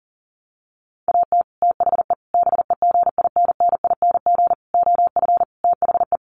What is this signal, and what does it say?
Morse code, heard as a keyed tone, about 40 wpm, 700 Hz. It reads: AT THE BEGINNING OF THE
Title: Sets of 5 Words 40wpm